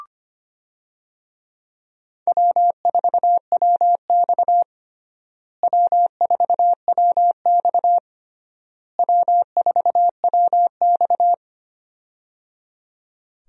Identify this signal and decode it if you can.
Morse code, heard as a keyed tone, about 25 wpm, 700 Hz. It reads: W4WX W4WX W4WX
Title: USA Call Signs - 4 Characters - Repeated 3x - 25wpm